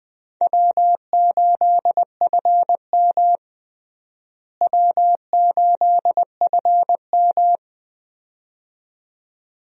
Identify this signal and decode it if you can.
Morse code, heard as a keyed tone, about 20 wpm, 700 Hz. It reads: W8FM W8FM
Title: USA Call Signs - 4 Characters - Repeated 2x - 20wpm